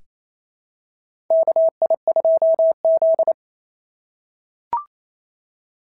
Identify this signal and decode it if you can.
Morse code, heard as a keyed tone, about 28 wpm, 690 Hz. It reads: KI2Z
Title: SOTA QSO Practice 28wpm